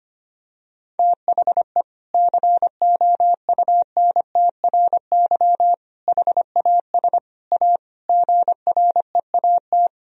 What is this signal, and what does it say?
Morse code, heard as a keyed tone, about 25 wpm, 700 Hz. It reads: THE COUNTRY HAS A GREAT
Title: Sentences from Top 1000 Words - Review Part 4 - 25wpm